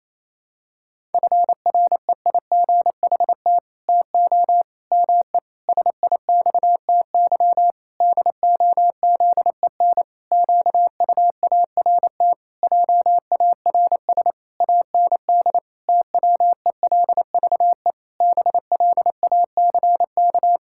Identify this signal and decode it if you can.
Morse code, heard as a keyed tone, about 28 wpm, 700 Hz. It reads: FREIGHT TO ME SIXTY DOZEN QUART JARS AND TWELVE BLACK
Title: Pangram Challenge v2 - 28wpm